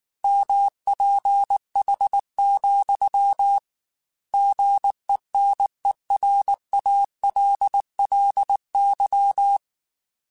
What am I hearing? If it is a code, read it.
Morse code, heard as a keyed tone, about 19 wpm, 800 Hz. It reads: MPH, GENERALLY
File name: Weather Sets / Weather Sets-mp3s / Weather 4 21 @18